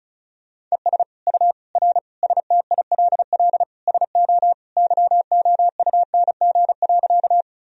Morse code, heard as a keyed tone, about 36 wpm, 700 Hz. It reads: ES U R STILL SO YOUNG.
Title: Sentences from Top 400 Words and Abbreviations 35wpm